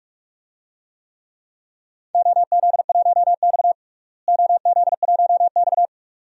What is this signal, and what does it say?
Morse code, heard as a keyed tone, about 45 wpm, 700 Hz. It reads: OZ1X OZ1X
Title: DXCC-100 Call Signs - 4 Characters - Repeated 2x - 45wpm